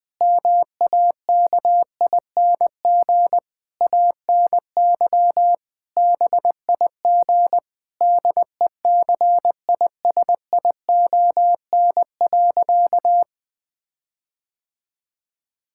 Morse code, heard as a keyed tone, about 20 wpm, 700 Hz. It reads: MAKING ANY BIG DECISION.
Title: Difficult Sentences with Hints - 20wpm